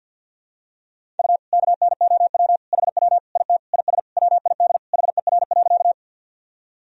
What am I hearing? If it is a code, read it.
Morse code, heard as a keyed tone, about 51 wpm, 700 Hz. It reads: U KNOW HW IT IS WID HER.
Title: Sentences from Top 100 Words and Abbreviations 50wpm